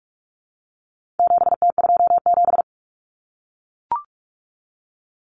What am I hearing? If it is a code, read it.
Morse code, heard as a keyed tone, about 44 wpm, 700 Hz. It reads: 7T27
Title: Common Cut Numbers - 4-Digit 45wpm